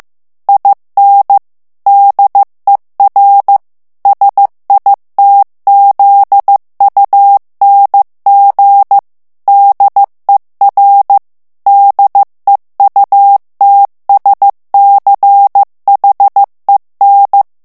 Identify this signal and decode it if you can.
Morse code, heard as a keyed tone, about 15 wpm, 800 Hz. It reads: IN DER SITZUNG DER DEUTSCHEN
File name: ALLEMAND texte clair 1 vitesse 14 mots minute